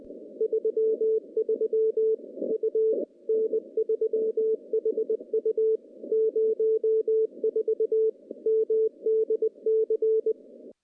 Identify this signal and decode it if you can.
Morse code, heard as a keyed tone, about 21 wpm, 440 Hz. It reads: 33 U N3HU 04 MDC
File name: contests / CW-SS-k5tr- -0000